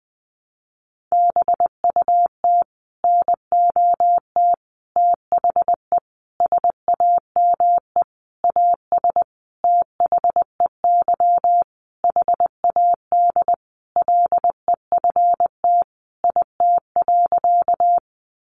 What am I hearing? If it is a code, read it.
Morse code, heard as a keyed tone, about 20 wpm, 700 Hz. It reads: BUT NOT THE SAME AS THEY HAD LEFT IT.
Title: A Sound Of Thunder 20wpm